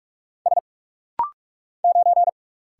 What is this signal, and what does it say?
Morse code, heard as a keyed tone, about 44 wpm, 700 Hz. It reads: S 9
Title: Single Letter-Number - Rapid-Fire 45wpm